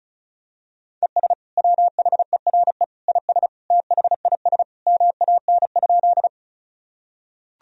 Morse code, heard as a keyed tone, about 36 wpm, 700 Hz. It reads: ES WHERE IS THIS MAN?
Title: Sentences from Top 200 Words and Abbreviations 35wpm